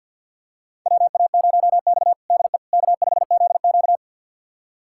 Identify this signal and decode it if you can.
Morse code, heard as a keyed tone, about 52 wpm, 700 Hz. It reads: WA0X DE K5ZQ
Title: USA Callsign DE Callsign - 4 Characters - 50wpm